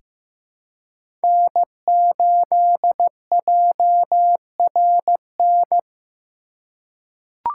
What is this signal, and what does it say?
Morse code, heard as a keyed tone, about 15 wpm, 700 Hz. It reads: N8JRN
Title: USA Call Signs - 5 Characters - 4x Character Spacing - 15wpm